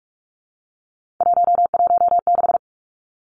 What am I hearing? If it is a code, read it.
Morse code, heard as a keyed tone, about 44 wpm, 700 Hz. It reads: 116
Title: Numbers - 3-Digit 45wpm